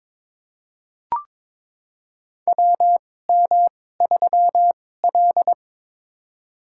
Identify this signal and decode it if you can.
Morse code, heard as a keyed tone, about 22 wpm, spaced as for 16 wpm, 700 Hz. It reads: WM3L
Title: USA Call Signs - 4 Characters - 2x Character Spacing - 22wpm